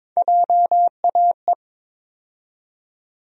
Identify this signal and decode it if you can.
Morse code, heard as a keyed tone, about 22 wpm, 700 Hz. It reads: JAE
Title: USA Call Sign Postfixes - 3 Letters 22wpm